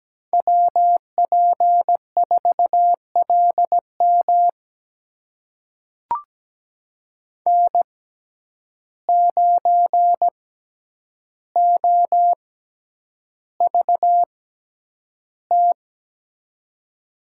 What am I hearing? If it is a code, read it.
Morse code, heard as a keyed tone, about 17 wpm, spaced as for 9 wpm, 700 Hz. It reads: WP4LM N9OVT
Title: USA Call Signs - 5 Characters - 6x Character Spacing - 17wpm